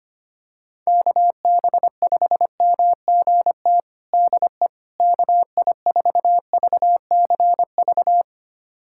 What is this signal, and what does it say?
Morse code, heard as a keyed tone, about 25 wpm, 700 Hz. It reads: KB5MGT DE KI4VCV
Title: USA Callsign DE Callsign - 6 Characters - 25wpm